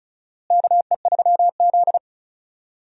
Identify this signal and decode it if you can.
Morse code, heard as a keyed tone, about 36 wpm, 700 Hz. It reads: KE3Z
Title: USA Call Signs - 4 Characters - Repeated 3x - 35wpm